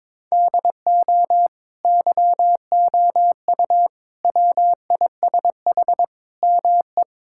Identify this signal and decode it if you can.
Morse code, heard as a keyed tone, about 22 wpm, 700 Hz. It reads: DO YOU WISH ME
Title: Sentences from Top 1600 Words 22wpm